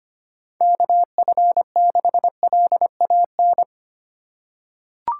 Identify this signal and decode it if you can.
Morse code, heard as a keyed tone, about 25 wpm, 700 Hz. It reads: KF6LAN
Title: USA Call Signs - 6 Characters - 2x Character Spacing - 25wpm